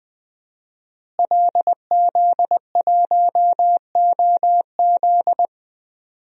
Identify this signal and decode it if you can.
Morse code, heard as a keyed tone, about 20 wpm, 700 Hz. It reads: LZ1OZ
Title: DXCC-100 Call Signs - 5 Characters - Repeated 2x - 20wpm